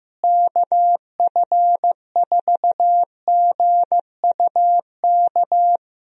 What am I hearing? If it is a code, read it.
Morse code, heard as a keyed tone, about 15 wpm, 700 Hz. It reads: KF4GUK
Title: USA Call Signs - 6 Characters - Repeated 2x - 15wpm